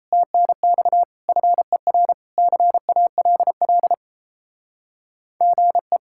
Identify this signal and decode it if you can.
Morse code, heard as a keyed tone, about 33 wpm, 700 Hz. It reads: TNX FER CALL GE
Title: Speed-Racing Common QSO Phrases - 28wpm